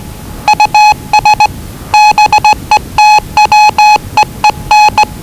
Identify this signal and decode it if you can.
Morse code, heard as a keyed tone, about 20 wpm, 840 Hz. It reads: US BETWEEN
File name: WellingtonRadioZlwCloseDownOnHfAndMf / 01-Zlw 4277khz 417.5 qrt until 0925z